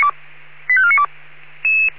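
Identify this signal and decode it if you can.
Five-tone selective call (ZVEI), heard as quick runs of five tones, tones of about 70 ms. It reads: 85492
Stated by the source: ZVEI five-tone selective call received on SDR.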